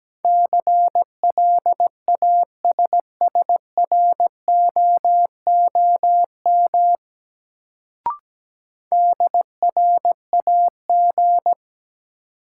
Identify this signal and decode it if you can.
Morse code, heard as a keyed tone, about 17 wpm, 700 Hz. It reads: CLASSROOM DRAG
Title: Sentences from Top 1500 Words - New Word Drill - 17wpm